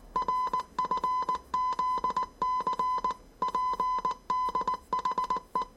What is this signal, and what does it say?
Morse code, heard as a keyed tone, about 19 wpm, 1010 Hz. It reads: RFZC PDHE
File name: 业余无线电 / ●未分类 / 字码80